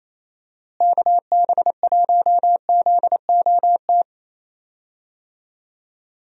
Morse code, heard as a keyed tone, about 28 wpm, 700 Hz. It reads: KB1ZOT
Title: USA Call Signs - 6 Characters - Repeated 3x - 28wpm